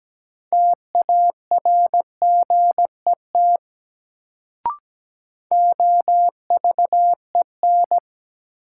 Morse code, heard as a keyed tone, about 17 wpm, 700 Hz. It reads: TARGET OVEN
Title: Sentences from Top 1200 Words - New Word Drill Alt - 17wpm